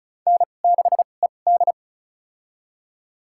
Morse code, heard as a keyed tone, about 36 wpm, spaced as for 25 wpm, 700 Hz. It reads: N6ED
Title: USA Call Signs - 4 Characters - 2x Character Spacing - 35wpm